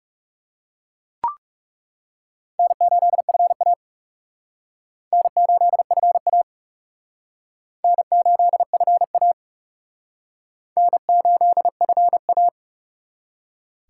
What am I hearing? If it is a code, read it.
Morse code, heard as a keyed tone, about 37 wpm, 700 Hz. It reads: N8FA N8FA N8FA N8FA
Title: Speed Racing - USA 1x2 Call Signs 30wpm